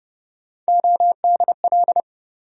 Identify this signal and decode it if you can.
Morse code, heard as a keyed tone, about 30 wpm, 700 Hz. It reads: ODL
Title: USA Call Sign Postfixes - 3 Letters 30wpm